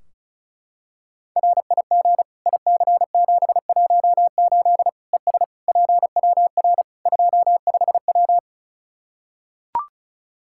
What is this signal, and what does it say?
Morse code, heard as a keyed tone, about 34 wpm, 700 Hz. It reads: RIG IC718 ES PWR 25W
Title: Practice QSO - Exchange 2 - Info - 35wpm